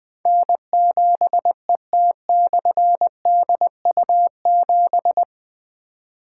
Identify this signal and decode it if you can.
Morse code, heard as a keyed tone, about 20 wpm, 700 Hz. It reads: N7ET/DU7
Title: Difficult Call Signs 20wpm